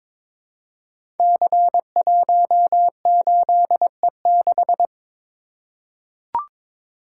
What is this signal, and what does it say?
Morse code, heard as a keyed tone, about 22 wpm, 700 Hz. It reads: C18E6
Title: Code Groups - 5 Letter:Numbers - 22wpm